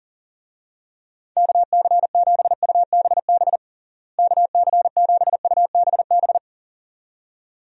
Morse code, heard as a keyed tone, about 40 wpm, 700 Hz. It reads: KC7UBB KC7UBB
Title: USA Call Signs - 6 Characters - Repeated 2x - 40wpm